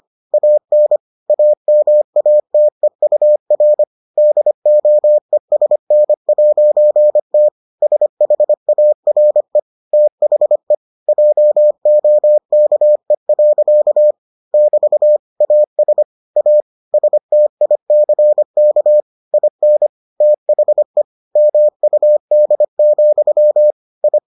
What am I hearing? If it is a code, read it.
Morse code, heard as a keyed tone, about 25 wpm, 600 Hz. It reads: AN AMATEUR DOESN'T SHARE THE JOKE. <BT> AS A STICK IN THE MUD, I